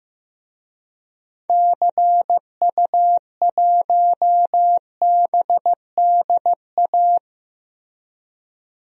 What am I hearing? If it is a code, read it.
Morse code, heard as a keyed tone, about 15 wpm, 700 Hz. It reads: CU1BDA
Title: DXCC-100 Call Signs - 6 Characters - Repeated 2x - 15wpm